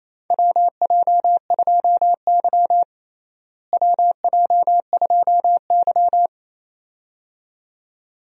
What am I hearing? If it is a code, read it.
Morse code, heard as a keyed tone, about 28 wpm, 700 Hz. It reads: WJ2Y WJ2Y
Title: USA Call Signs - 4 Characters - Repeated 2x - 28wpm